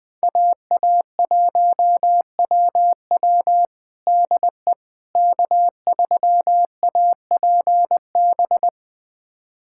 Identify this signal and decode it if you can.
Morse code, heard as a keyed tone, about 21 wpm, 700 Hz. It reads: AA1WW DE K3APB
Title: USA Callsign DE Callsign - 5 Characters - 20wpm